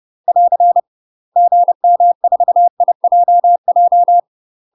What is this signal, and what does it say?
Morse code, heard as a keyed tone, about 30 wpm, 700 Hz. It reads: <AR> GM4IJJ
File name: UK MASTER QSOs TRAINING / 30WPM / UK MASTER QSOs 0041